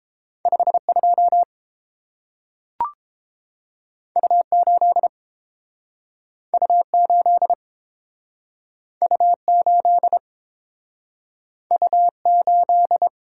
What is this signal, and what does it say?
Morse code, heard as a keyed tone, about 30 wpm, 700 Hz. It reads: 52 U8 U8 U8 U8